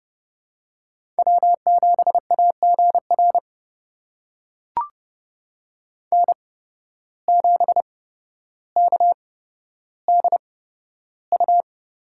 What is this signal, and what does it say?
Morse code, heard as a keyed tone, about 31 wpm, spaced as for 13 wpm, 700 Hz. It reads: W7AGR N7KDU
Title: USA Call Signs - 5 Characters - 8x Character Spacing - 30wpm